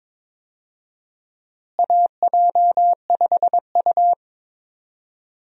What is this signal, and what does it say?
Morse code, heard as a keyed tone, about 22 wpm, 700 Hz. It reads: AJ5U
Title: USA Call Signs - 4 Characters - Repeated 3x - 22wpm